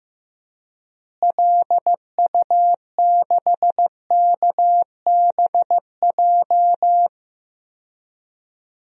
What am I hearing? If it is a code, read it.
Morse code, heard as a keyed tone, about 16 wpm, 700 Hz. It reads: LU6KBJ
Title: DXCC-100 Call Signs - 6 Characters - Repeated 3x - 15wpm